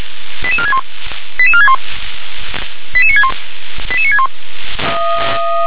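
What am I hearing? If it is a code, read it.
Five-tone selective call (ZVEI), heard as quick runs of five tones, tones of about 70 ms. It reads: 88461 88461 88961 88961
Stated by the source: ZVEI five-tone selective call received on SDR.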